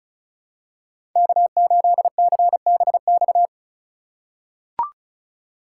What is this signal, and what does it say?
Morse code, heard as a keyed tone, about 34 wpm, 700 Hz. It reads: K8CBX
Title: USA Call Signs - 5 Characters - 6x Character Spacing - 35wpm